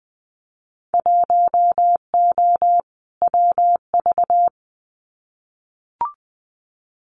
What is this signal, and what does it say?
Morse code, heard as a keyed tone, about 20 wpm, 700 Hz. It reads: 1O WV
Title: Winter Field Day Exchanges Encore 20wpm